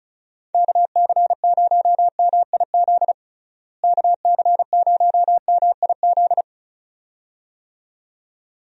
Morse code, heard as a keyed tone, about 36 wpm, 700 Hz. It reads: KC0MIZ KC0MIZ
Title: USA Call Signs - 6 Characters - Repeated 2x - 35wpm